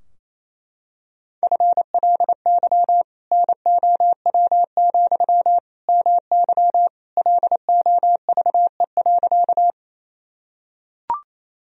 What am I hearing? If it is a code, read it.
Morse code, heard as a keyed tone, about 28 wpm, 700 Hz. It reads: FLY NOW, MY LOVE.
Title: Sentences from Top 1000 Words - Review Part 2 - 28wpm